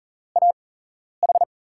Morse code, heard as a keyed tone, about 38 wpm, spaced as for 11 wpm, 700 Hz. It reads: AH
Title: USA Call Signs - 8x Character Spacing - 40wpm